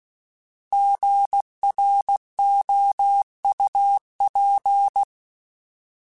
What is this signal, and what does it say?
Morse code, heard as a keyed tone, about 16 wpm, 800 Hz. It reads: GROUP
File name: Alphabetical / Alphabetical mp3s / Gs @15